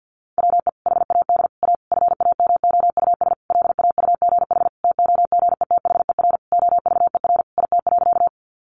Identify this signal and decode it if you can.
Morse code, heard as a keyed tone, about 50 wpm, 700 Hz. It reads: WE HAD A FAMOUS LAUGH TOGETHER OVER IT.
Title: Sentences from Top 1500 Words 50wpm